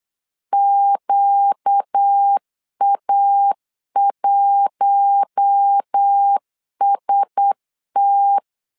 Morse code, heard as a keyed tone, about 8 wpm, 800 Hz. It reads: QA1ST
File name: LEZIONI / 4 Q1S3 8wpm 3037